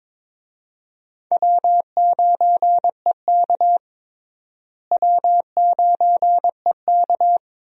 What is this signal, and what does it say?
Morse code, heard as a keyed tone, about 22 wpm, 700 Hz. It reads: W9EK W9EK
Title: USA Call Signs - 4 Characters - Repeated 2x - 22wpm